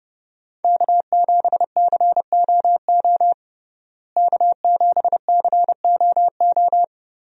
Morse code, heard as a keyed tone, about 32 wpm, 700 Hz. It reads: K7COO K7COO
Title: USA Call Signs - 5 Characters - Repeated 3x - 30wpm